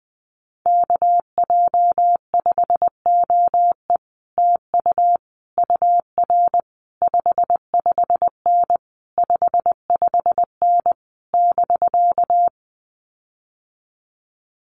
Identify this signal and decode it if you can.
Morse code, heard as a keyed tone, about 20 wpm, 700 Hz. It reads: KJ5OE TU UR 55N 55N <BK>
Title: Common QSO Phrases Encore 20wpm